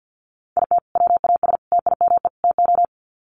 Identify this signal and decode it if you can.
Morse code, heard as a keyed tone, about 48 wpm, 700 Hz. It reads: IT WAS TIME TO
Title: Sets of 4 Words 50wpm